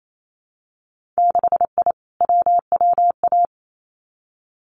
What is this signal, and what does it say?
Morse code, heard as a keyed tone, about 28 wpm, 690 Hz. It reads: 6I WWA
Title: Winter Field Day Exchanges Encore 28wpm